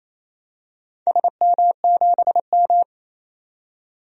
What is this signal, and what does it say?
Morse code, heard as a keyed tone, about 28 wpm, 700 Hz. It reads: SM7M
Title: DXCC-100 Call Signs - 4 Characters - 28wpm